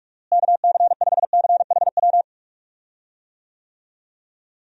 Morse code, heard as a keyed tone, about 44 wpm, 700 Hz. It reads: KC5CHW
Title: USA Call Signs - 6 Characters - Repeated 3x - 45wpm